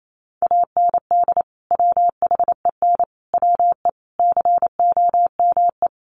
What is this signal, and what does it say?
Morse code, heard as a keyed tone, about 28 wpm, 700 Hz. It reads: AND WHEN WE COME
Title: Sets of 5 Words 28wpm